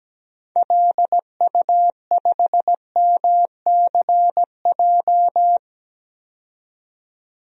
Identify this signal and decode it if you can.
Morse code, heard as a keyed tone, about 17 wpm, 700 Hz. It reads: LU5MCJ
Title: DXCC-100 Call Signs - 6 Characters - 17wpm